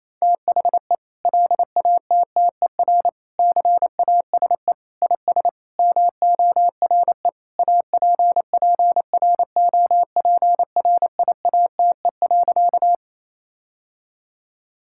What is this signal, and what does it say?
Morse code, heard as a keyed tone, about 28 wpm, 700 Hz. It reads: THE LATTER CASE IS MORE APPROPRIATE.
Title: Sentences from Top 1600 Words 28wpm